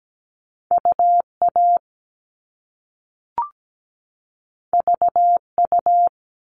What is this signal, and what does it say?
Morse code, heard as a keyed tone, about 17 wpm, 700 Hz. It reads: UA VU
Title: All Cut Numbers - 2-Digit 17wpm